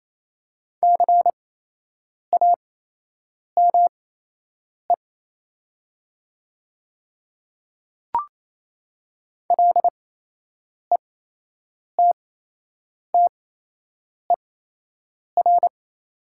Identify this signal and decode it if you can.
Morse code, heard as a keyed tone, about 28 wpm, spaced as for 8 wpm, 700 Hz. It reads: CAME LETTER
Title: 8x Character Spacing - Top 300 Words - 28wpm